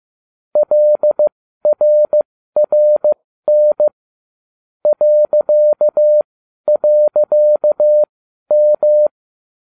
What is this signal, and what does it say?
Morse code, heard as a keyed tone, about 15 wpm, spaced as for 12 wpm, 600 Hz. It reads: LRRN ..M
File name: cw / ubacw34 3